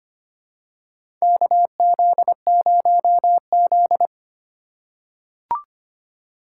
Morse code, heard as a keyed tone, about 25 wpm, 700 Hz. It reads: KZ0Z
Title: USA Call Signs - 4 Characters - 8x Character Spacing - 25wpm